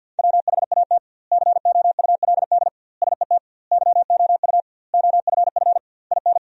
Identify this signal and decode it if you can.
Morse code, heard as a keyed tone, about 52 wpm, 700 Hz. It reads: WHAT COULD SET YOU OFF IN
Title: 2x Word-Spacing - Sentences from Top 200 Words - 50wpm